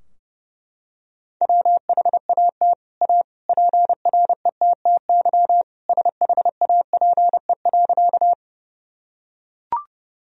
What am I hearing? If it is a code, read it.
Morse code, heard as a keyed tone, about 30 wpm, 700 Hz. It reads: WHAT A PRETTY SHAPE.
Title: Sentences from Top 1000 Words - Review Part 3 - 30wpm